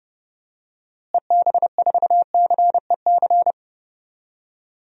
Morse code, heard as a keyed tone, about 30 wpm, 700 Hz. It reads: EB4CEC
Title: DXCC-100 Call Signs - 6 Characters - Repeated 3x - 30wpm